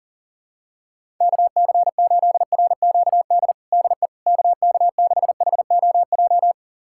Morse code, heard as a keyed tone, about 40 wpm, 700 Hz. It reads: KC8RQD DE KK6HOJ
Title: USA Callsign DE Callsign - 6 Characters - 40wpm